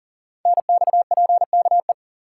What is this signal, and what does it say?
Morse code, heard as a keyed tone, about 40 wpm, 700 Hz. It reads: NXPKE
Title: Code Groups - 5 Letters - 40wpm